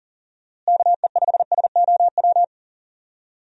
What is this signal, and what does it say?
Morse code, heard as a keyed tone, about 37 wpm, 700 Hz. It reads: KE5SOW
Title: USA Call Signs - 6 Characters - Repeated 3x - 40wpm